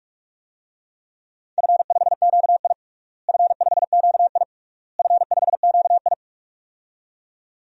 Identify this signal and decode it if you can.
Morse code, heard as a keyed tone, about 46 wpm, 700 Hz. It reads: F5QI F5QI F5QI
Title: DXCC-100 Call Signs - 4 Characters - Repeated 3x - 45wpm